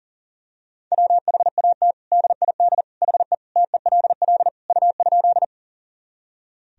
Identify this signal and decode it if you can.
Morse code, heard as a keyed tone, about 40 wpm, 700 Hz. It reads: WHAT DID HE TELL U?
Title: Sentences from Top 200 Words and Abbreviations 40wpm